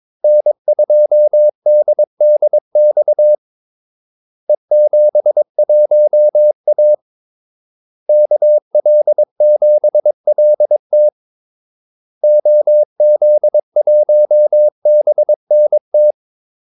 Morse code, heard as a keyed tone, about 22 wpm, 600 Hz. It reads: N2DDX E71A KL7LT OZ1BNT
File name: Callsigns-22wpm